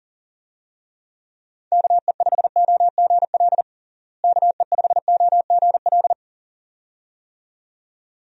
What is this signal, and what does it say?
Morse code, heard as a keyed tone, about 40 wpm, 700 Hz. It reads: KE5OGL KE5OGL
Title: USA Call Signs - 6 Characters - Repeated 2x - 40wpm